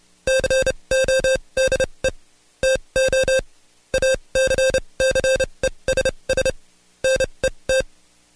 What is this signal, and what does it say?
Morse code, heard as a keyed tone, about 29 wpm, spaced as for 23 wpm, 530 Hz. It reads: CODE TO ACCESS NET
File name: Code Quick / AudioDownld2 / Seg138-Stepper11 23WPM